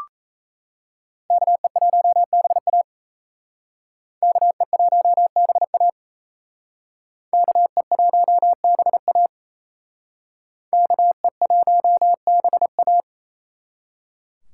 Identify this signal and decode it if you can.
Morse code, heard as a keyed tone, about 36 wpm, 700 Hz. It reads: KE1BA KE1BA KE1BA KE1BA